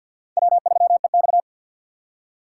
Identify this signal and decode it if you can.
Morse code, heard as a keyed tone, about 52 wpm, 700 Hz. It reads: W3EX
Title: USA Call Signs - 4 Characters - 3x Character Spacing - 50wpm